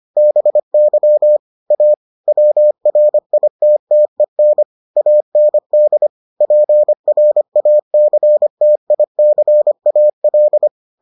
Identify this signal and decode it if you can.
Morse code, heard as a keyed tone, about 25 wpm, 600 Hz. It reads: BY A WRITTEN AND PRACTICAL